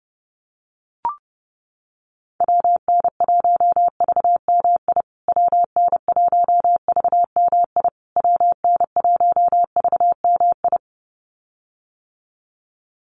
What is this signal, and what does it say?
Morse code, heard as a keyed tone, about 30 wpm, 700 Hz. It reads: WN1VMI WN1VMI WN1VMI
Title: USA Call Signs Repeated 3x 30wpm